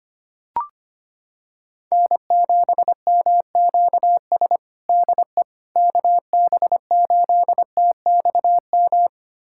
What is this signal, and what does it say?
Morse code, heard as a keyed tone, about 25 wpm, 700 Hz. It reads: N7MQS DE KB8TXM
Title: USA Callsign DE Callsign 25wpm